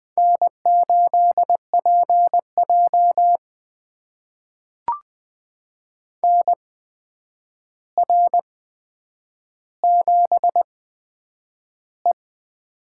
Morse code, heard as a keyed tone, about 20 wpm, spaced as for 9 wpm, 700 Hz. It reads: N8PJ NR7E
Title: USA Call Signs - 4 Characters - 8x Character Spacing - 20wpm